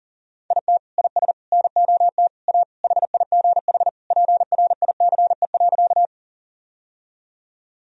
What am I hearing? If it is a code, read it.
Morse code, heard as a keyed tone, about 40 wpm, 700 Hz. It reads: IT IS NOT A HIGH PRICE.
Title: Sentences from Top 1000 Words - Review Part 4 - 40wpm